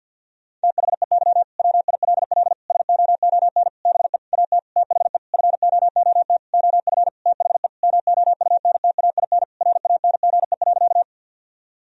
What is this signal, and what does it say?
Morse code, heard as a keyed tone, about 50 wpm, 700 Hz. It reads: THEY WILL SOON BE AT THE FOOT OF THE MOUNTAIN RANGE.
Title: Sentences from Top 1000 Words - Review Part 4 - 50wpm